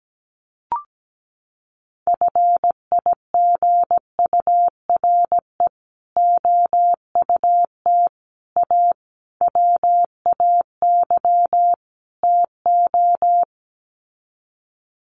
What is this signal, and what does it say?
Morse code, heard as a keyed tone, about 17 wpm, 700 Hz. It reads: FIGURE OUT A WAY TO
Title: Sets of 5 Words Encore2 17wpm